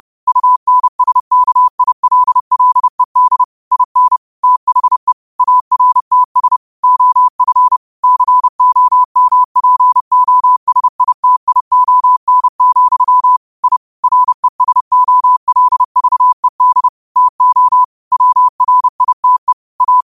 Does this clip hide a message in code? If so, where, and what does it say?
Yes: Morse code from 0.3 to 20.0 s, heard as a keyed tone, about 30 wpm, 1000 Hz. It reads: UNSKILLED IN THE ARTS OF COMPOSITION, I RESOLVED TO WRITE A